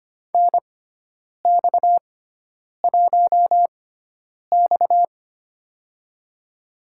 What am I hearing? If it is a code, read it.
Morse code, heard as a keyed tone, about 25 wpm, spaced as for 9 wpm, 700 Hz. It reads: NX1X
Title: USA Call Signs - 4 Characters - 6x Character Spacing - 25wpm